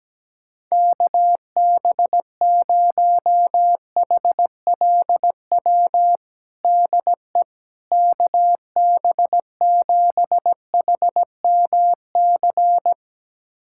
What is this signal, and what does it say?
Morse code, heard as a keyed tone, about 17 wpm, 700 Hz. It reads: KB0HLW DE KB7HMC
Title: USA Callsign DE Callsign - 6 Characters - 17wpm